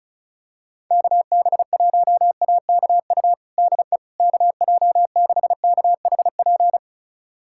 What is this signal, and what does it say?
Morse code, heard as a keyed tone, about 36 wpm, 700 Hz. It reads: KB1AKU DE KJ6KHP
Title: USA Callsign DE Callsign - 6 Characters - 35wpm